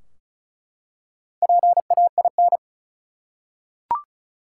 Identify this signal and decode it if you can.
Morse code, heard as a keyed tone, about 36 wpm, 700 Hz. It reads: PAIN
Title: Four Letter Words - 35wpm